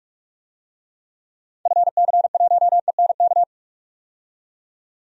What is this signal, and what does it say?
Morse code, heard as a keyed tone, about 47 wpm, 700 Hz. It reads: FC1ENK
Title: DXCC-100 Call Signs - 6 Characters - 45wpm